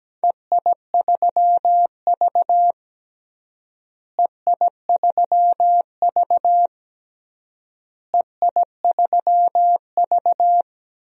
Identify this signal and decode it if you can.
Morse code, heard as a keyed tone, about 17 wpm, 700 Hz. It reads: EI3V EI3V EI3V
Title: DXCC-100 Call Signs - 4 Characters - Repeated 3x - 17wpm